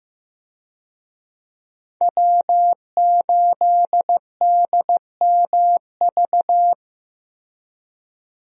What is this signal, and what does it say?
Morse code, heard as a keyed tone, about 15 wpm, 700 Hz. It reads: W8DMV
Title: USA Call Signs - 5 Characters - Repeated 2x - 15wpm